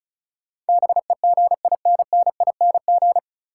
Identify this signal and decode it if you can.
Morse code, heard as a keyed tone, about 34 wpm, 700 Hz. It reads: BEGINNING
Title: Sentences from Top 800 Words - New Word Drill - 35wpm